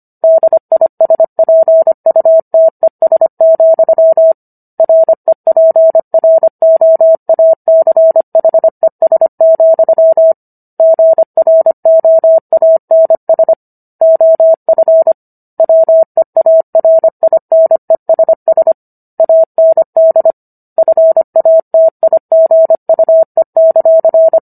Morse code, heard as a keyed tone, about 25 wpm, 650 Hz. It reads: DISPUTES, REPROACHES, GROANS OF WEARINESS AND FATIGUE;
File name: Book One / 16-Chapter 16 - 25 WPM